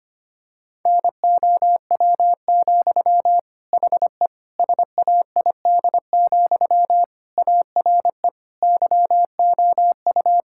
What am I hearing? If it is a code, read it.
Morse code, heard as a keyed tone, about 25 wpm, 700 Hz. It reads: NOW, HE SAID, ARE YOU
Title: Sentences from Top 1600 Words 25wpm